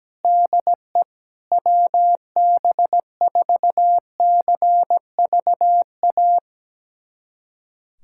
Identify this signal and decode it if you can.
Morse code, heard as a keyed tone, about 17 wpm, 700 Hz. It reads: DE WB4CVA
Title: USA Callsign DE Callsign 17wpm